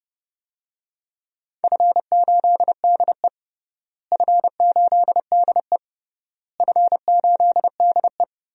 Morse code, heard as a keyed tone, about 32 wpm, 700 Hz. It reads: F8DE F8DE F8DE
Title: DXCC-100 Call Signs - 4 Characters - Repeated 3x - 30wpm